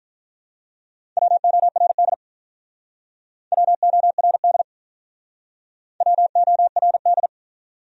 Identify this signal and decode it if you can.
Morse code, heard as a keyed tone, about 48 wpm, 700 Hz. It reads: WORD WORD WORD